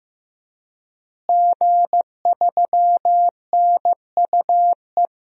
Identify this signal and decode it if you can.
Morse code, heard as a keyed tone, about 15 wpm, 700 Hz. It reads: G3NUE
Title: DXCC-100 Call Signs - 5 Characters - Repeated 3x - 15wpm